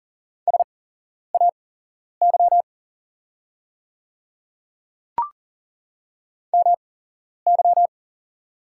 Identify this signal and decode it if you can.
Morse code, heard as a keyed tone, about 38 wpm, spaced as for 11 wpm, 700 Hz. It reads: SAY MY
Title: 8x Character Spacing - Top 200 Words - 40wpm